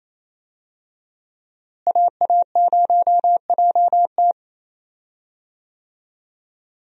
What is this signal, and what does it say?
Morse code, heard as a keyed tone, about 28 wpm, 700 Hz. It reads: AA0JT
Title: USA Call Signs - 5 Characters - Repeated 2x - 28wpm